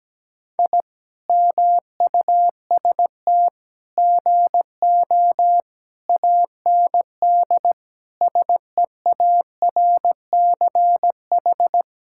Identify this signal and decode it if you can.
Morse code, heard as a keyed tone, about 17 wpm, 700 Hz. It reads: I MUST GO AND SEARCH
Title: Sentences from Top 1000 Words - Review Part 5 - 17wpm